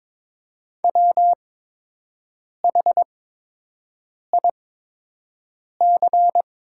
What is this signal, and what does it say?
Morse code, heard as a keyed tone, about 22 wpm, spaced as for 6 wpm, 700 Hz. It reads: WHIC
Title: 8x Character Spacing - Top 300 Words - 22wpm